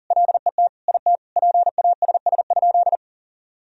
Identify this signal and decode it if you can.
Morse code, heard as a keyed tone, about 40 wpm, 700 Hz. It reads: LET IT PASS?
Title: Sentences from Top 400 Words and Abbreviations 40wpm